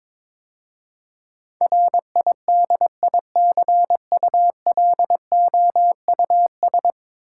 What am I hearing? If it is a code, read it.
Morse code, heard as a keyed tone, about 21 wpm, 700 Hz. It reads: RIDICULOUS
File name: practice / 22wpm / 10 Letter Words - First 3 Letters Given - 22wpm